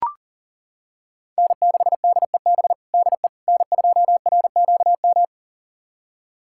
Morse code, heard as a keyed tone, about 40 wpm, 700 Hz. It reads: N6DEB DE N2RQM
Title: USA Callsign DE Callsign - 5 Characters - 40wpm